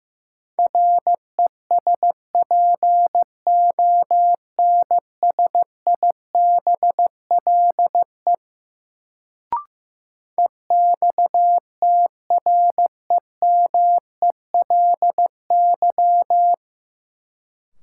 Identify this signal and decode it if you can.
Morse code, heard as a keyed tone, about 15 wpm, 700 Hz. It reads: RESPONSIBLE EXTREMELY
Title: Sentences from Top 1000 Words - Review Part 7 - New Word Drill - 15wpm